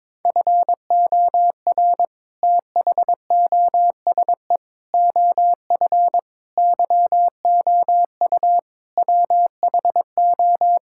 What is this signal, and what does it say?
Morse code, heard as a keyed tone, about 22 wpm, 700 Hz. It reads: FOR THOSE OF YOU WHO
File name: practice / 22wpm / Sets of 7 Words - 22wpm